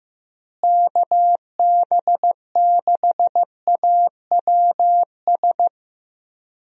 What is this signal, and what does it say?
Morse code, heard as a keyed tone, about 15 wpm, 700 Hz. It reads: KB6AWS
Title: USA Call Signs - 6 Characters - Repeated 3x - 15wpm